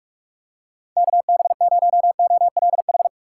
Morse code, heard as a keyed tone, about 44 wpm, 700 Hz. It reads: KB0OLH
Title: USA Call Signs - 6x Character Spacing - 45wpm